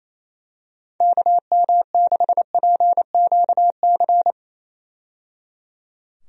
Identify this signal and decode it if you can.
Morse code, heard as a keyed tone, about 28 wpm, 700 Hz. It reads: KM6PQC
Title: USA Call Signs - 6 Characters - 28wpm